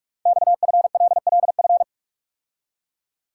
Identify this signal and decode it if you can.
Morse code, heard as a keyed tone, about 44 wpm, 690 Hz. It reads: XFLLF
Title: Code Groups - 5 Letters - 45wpm